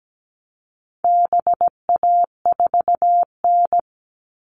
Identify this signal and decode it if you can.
Morse code, heard as a keyed tone, about 17 wpm, 700 Hz. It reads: BA4N
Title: All Cut Numbers - 4-Digit 17wpm